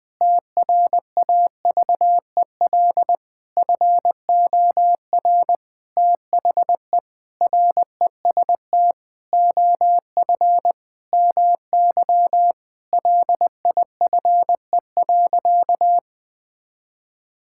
Morse code, heard as a keyed tone, about 20 wpm, 700 Hz. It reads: TRAVEL FOR THE REST OF MY LIFE.
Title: Sentences with 5-Word Phrases - 20wpm